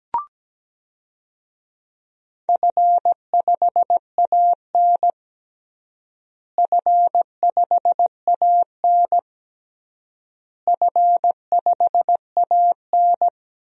Morse code, heard as a keyed tone, about 17 wpm, 700 Hz. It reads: F5AN F5AN F5AN
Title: DXCC-100 Call Signs - 4 Characters - Repeated 3x - 17wpm